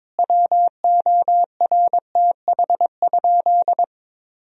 Morse code, heard as a keyed tone, about 22 wpm, 700 Hz. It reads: WORTH?
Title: Sentences from Top 1000 Words - Review Part 4 - 22wpm